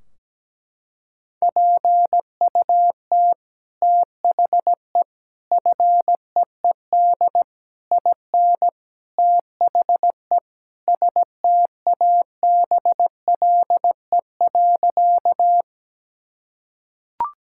Morse code, heard as a keyed tone, about 17 wpm, 700 Hz. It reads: PUT THE FEED IN THE STABLE.
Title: Sentences from Top 1600 Words 17wpm